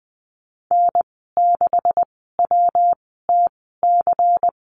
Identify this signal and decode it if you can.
Morse code, heard as a keyed tone, about 20 wpm, spaced as for 15 wpm, 700 Hz. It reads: N6WTC
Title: 173 Lesson 28g - Number 9 - Call Signs 20wpm